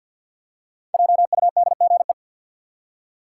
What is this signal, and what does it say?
Morse code, heard as a keyed tone, about 53 wpm, 700 Hz. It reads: JUDGE
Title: Five Letter Words - 50wpm